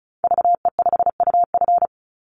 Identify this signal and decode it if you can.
Morse code, heard as a keyed tone, about 34 wpm, 700 Hz. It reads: VE5UF
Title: Call Signs - 35wpm